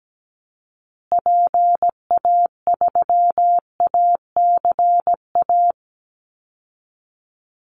Morse code, heard as a keyed tone, about 17 wpm, 700 Hz. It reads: PA3ACA
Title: Call Signs Repeated 3x - 17wpm